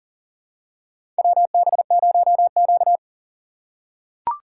Morse code, heard as a keyed tone, about 40 wpm, 700 Hz. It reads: WB0Q
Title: USA Call Signs - 4 Characters - 6x Character Spacing - 40wpm